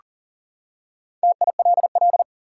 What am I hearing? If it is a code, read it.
Morse code, heard as a keyed tone, about 40 wpm, 700 Hz. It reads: TILL
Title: Four Letter Words - 40wpm